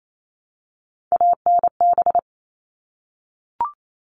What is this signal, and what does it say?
Morse code, heard as a keyed tone, about 28 wpm, 700 Hz. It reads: ANB
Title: All Cut Numbers - 3-Digit 28wpm